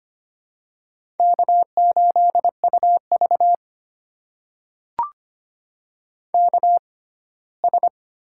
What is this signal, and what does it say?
Morse code, heard as a keyed tone, about 25 wpm, spaced as for 17 wpm, 700 Hz. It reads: K8UV KS
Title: USA Call Signs - 4 Characters - 6x Character Spacing - 25wpm